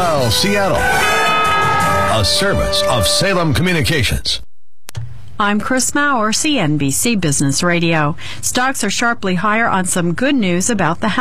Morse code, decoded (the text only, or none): none